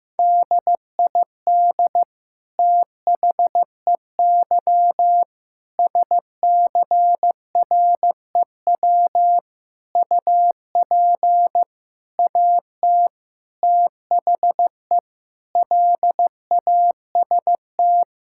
Morse code, heard as a keyed tone, about 15 wpm, 700 Hz. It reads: DID THEY SCREW UP AT THE LAST
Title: Sentences from Top 1600 Words 15wpm